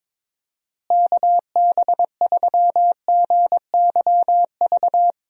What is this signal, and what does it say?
Morse code, heard as a keyed tone, about 22 wpm, 700 Hz. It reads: KB3GYV
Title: USA Call Signs - 6 Characters - Repeated 2x - 22wpm